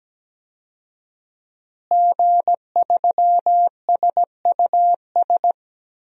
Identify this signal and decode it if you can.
Morse code, heard as a keyed tone, about 17 wpm, 700 Hz. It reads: G3SUS
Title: DXCC-100 Call Signs - 5 Characters - Repeated 2x - 17wpm